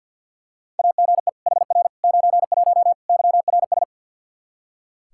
Morse code, heard as a keyed tone, about 46 wpm, 700 Hz. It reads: AGE HR 91 YRS
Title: K1USN SST QSO Practice 50wpm